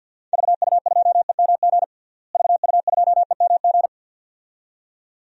Morse code, heard as a keyed tone, about 50 wpm, 700 Hz. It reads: VU2EMG VU2EMG
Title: DXCC-100 Call Signs - 6 Characters - Repeated 2x - 50wpm